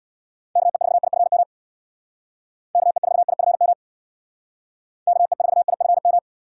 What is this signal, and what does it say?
Morse code, heard as a keyed tone, about 60 wpm, 700 Hz. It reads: KE4IFM KE4IFM KE4IFM
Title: Speed Racing - USA 2x3 Call Signs 50wpm